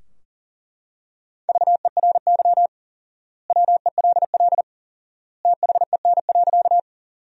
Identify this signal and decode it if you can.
Morse code, heard as a keyed tone, about 40 wpm, 700 Hz. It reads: VERY WELL THEN.
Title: 4x Word-Spacing - Sentences from Top 200 Words - 40wpm